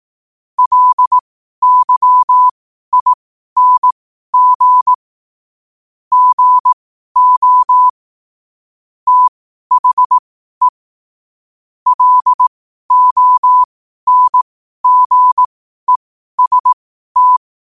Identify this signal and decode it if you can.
Morse code, heard as a keyed tone, about 18 wpm, spaced as for 12 wpm, 1000 Hz. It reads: LYING GO THE LONGEST
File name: qotdfiles / 12wpm